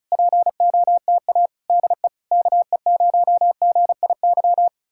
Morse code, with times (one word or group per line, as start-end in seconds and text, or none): 0.1-1.5 POTA
1.7-2.1 DE
2.3-4.7 KE0GIY